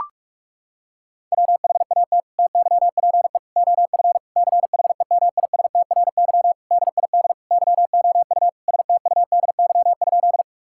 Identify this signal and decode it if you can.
Morse code, heard as a keyed tone, about 42 wpm, 700 Hz. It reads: WHAT TYPE OF CHEMISTRY DID YOU STUDY?
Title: Sentences from Top 1000 Words - Review Part 6 - 45wpm